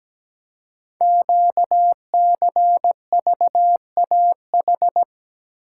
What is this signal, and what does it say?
Morse code, heard as a keyed tone, about 17 wpm, 700 Hz. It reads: QCVAH
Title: Code Groups - 5 Letters - 17wpm